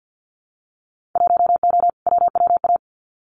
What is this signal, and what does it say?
Morse code, heard as a keyed tone, about 50 wpm, 700 Hz. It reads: 1O WWA
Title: Winter Field Day Exchanges 50wpm